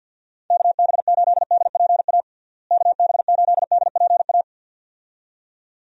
Morse code, heard as a keyed tone, about 50 wpm, 700 Hz. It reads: KB8DPA KB8DPA
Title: USA Call Signs - 6 Characters - Repeated 2x - 50wpm